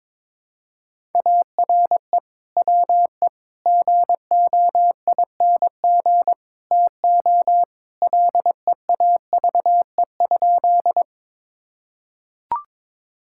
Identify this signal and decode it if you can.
Morse code, heard as a keyed tone, about 22 wpm, 700 Hz. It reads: ARE WE GOING TO LEAVE?
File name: practice / 22wpm / Sentences with 4-Word Phrases - 22wpm